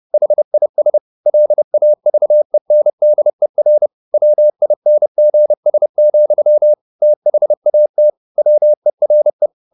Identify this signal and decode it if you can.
Morse code, heard as a keyed tone, about 30 wpm, 600 Hz. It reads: HIS LAVENDER WINGS, THAT WERE
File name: the old man and the sea / the old man and the sea 1 30wpm